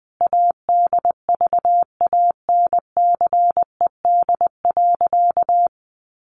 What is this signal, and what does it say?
Morse code, heard as a keyed tone, about 20 wpm, 700 Hz. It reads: ADVANCED.
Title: Sentences from Top 1200 Words 20wpm